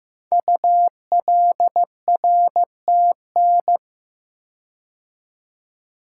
Morse code, heard as a keyed tone, about 15 wpm, 700 Hz. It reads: ULRTN
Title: Code Groups - 5 Letters - 15wpm